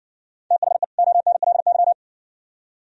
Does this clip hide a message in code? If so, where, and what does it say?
Yes: Morse code from 0.5 to 1.9 s, heard as a keyed tone, about 58 wpm, 700 Hz. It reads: THE ONLY